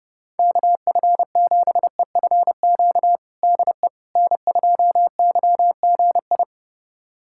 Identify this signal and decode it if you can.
Morse code, heard as a keyed tone, about 30 wpm, 700 Hz. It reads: KF7EFQ DE N2YGI
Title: USA Callsign DE Callsign 30wpm